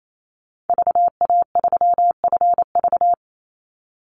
Morse code, heard as a keyed tone, about 28 wpm, 700 Hz. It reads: VA3FV
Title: Call Signs Repeated 2x - 28wpm